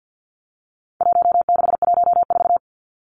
Morse code, heard as a keyed tone, about 50 wpm, 700 Hz. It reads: 1614
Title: Numbers - Contest Style - 4-Digit 50wpm